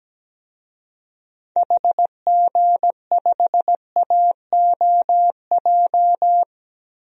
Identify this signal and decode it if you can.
Morse code, heard as a keyed tone, about 17 wpm, 700 Hz. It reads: HG5AOJ
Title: DXCC-100 Call Signs - 6 Characters - 17wpm